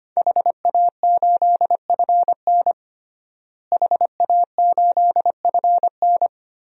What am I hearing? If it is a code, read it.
Morse code, heard as a keyed tone, about 25 wpm, 700 Hz. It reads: HA8FN HA8FN
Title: DXCC-100 Call Signs - 5 Characters - Repeated 3x - 25wpm